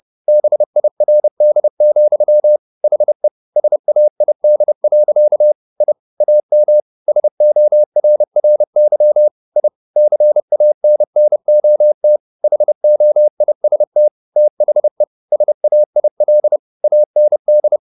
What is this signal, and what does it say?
Morse code, heard as a keyed tone, about 30 wpm, 600 Hz. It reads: BIRD, HE SAID. I AM SORRY I CANNOT HOIST THE SAIL AND
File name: the old man and the sea / the old man and the sea 1 30wpm